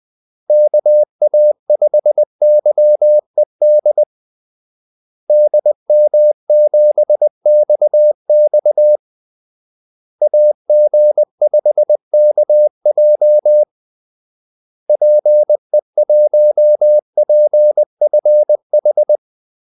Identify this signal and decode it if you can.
Morse code, heard as a keyed tone, about 20 wpm, 600 Hz. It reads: KA5YED DM7XX AG5KJ PE1PFH
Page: Kaldesignaler 20 wpm